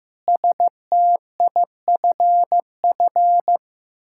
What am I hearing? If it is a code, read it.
Morse code, heard as a keyed tone, about 15 wpm, 700 Hz. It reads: STIFF
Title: Pangram Challenge v2 - 15wpm